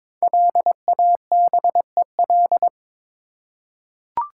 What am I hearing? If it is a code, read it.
Morse code, heard as a keyed tone, about 22 wpm, 700 Hz. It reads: LABEL
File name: 5 Letter Words - First 3 Letters Given - 22wpm